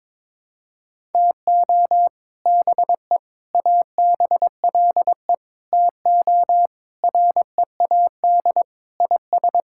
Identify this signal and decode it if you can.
Morse code, heard as a keyed tone, about 22 wpm, 700 Hz. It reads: TO BE ABLE TO READ IS
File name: practice / 22wpm / Sentences with 4-Word Phrases - 22wpm